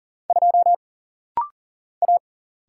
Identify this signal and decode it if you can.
Morse code, heard as a keyed tone, about 41 wpm, 700 Hz. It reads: 2 A
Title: Single Letter-Number - Rapid-Fire 40wpm